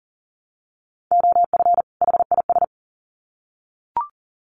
Morse code, heard as a keyed tone, about 40 wpm, 700 Hz. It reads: OF HIS
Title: Sets of 2 Words 40wpm